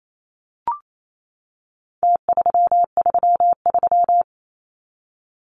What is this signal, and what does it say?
Morse code, heard as a keyed tone, about 28 wpm, 700 Hz. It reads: T333
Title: Numbers - Contest Style - 4-Digit 28wpm